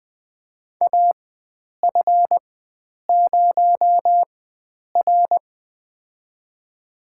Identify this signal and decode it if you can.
Morse code, heard as a keyed tone, about 20 wpm, spaced as for 9 wpm, 700 Hz. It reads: AF0R
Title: USA Call Signs - 4 Characters - 4x Character Spacing - 20wpm